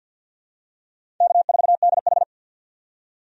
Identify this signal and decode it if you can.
Morse code, heard as a keyed tone, about 53 wpm, 700 Hz. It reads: K4DH
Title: USA Call Signs - 4 Characters - 50wpm